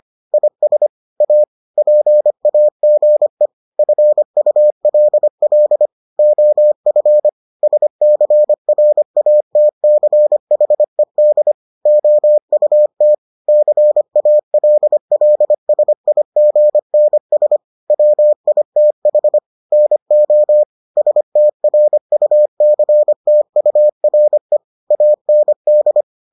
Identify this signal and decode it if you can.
Morse code, heard as a keyed tone, about 25 wpm, 600 Hz. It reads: IS A PAGE FULL OF SCRATCHED OUT CALLSIGNS WITH NO STRUCTURE AND